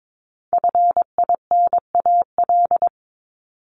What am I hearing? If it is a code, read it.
Morse code, heard as a keyed tone, about 22 wpm, 700 Hz. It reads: FINAL
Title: Top 500 Words Repeated 22wpm